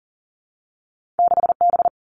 Morse code, heard as a keyed tone, about 40 wpm, 700 Hz. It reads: 6 B
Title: Two Challenging Letter-Number Combinations 40wpm